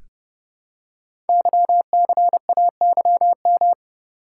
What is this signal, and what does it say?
Morse code, heard as a keyed tone, about 30 wpm, 700 Hz. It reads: YCAYM
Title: Code Groups - 5 Letters - 30wpm